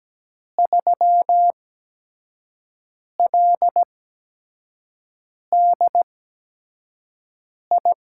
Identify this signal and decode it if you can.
Morse code, heard as a keyed tone, about 17 wpm, spaced as for 5 wpm, 700 Hz. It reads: 3LDI
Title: USA Call Signs - 6 Characters - 8x Character Spacing - 17wpm